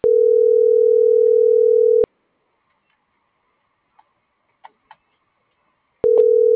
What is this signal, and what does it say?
Ringback tone (North America), 440 + 480 Hz.